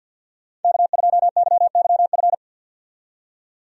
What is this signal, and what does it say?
Morse code, heard as a keyed tone, about 51 wpm, 700 Hz. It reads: K2YYF
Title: USA Call Signs - 5 Characters - 50wpm